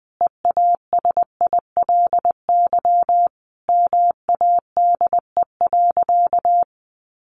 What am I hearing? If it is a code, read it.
Morse code, heard as a keyed tone, about 20 wpm, 700 Hz. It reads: EASILY MADE.
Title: Sentences from Top 1200 Words 20wpm